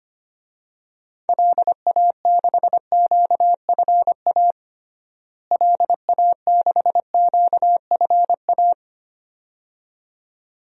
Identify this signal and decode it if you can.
Morse code, heard as a keyed tone, about 25 wpm, 700 Hz. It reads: LA6QFA LA6QFA
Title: DXCC-100 Call Signs - 6 Characters - Repeated 2x - 25wpm